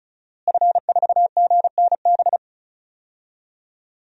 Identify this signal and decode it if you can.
Morse code, heard as a keyed tone, about 34 wpm, 700 Hz. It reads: F4GNB
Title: DXCC-100 Call Signs - 5 Characters - Repeated 2x - 35wpm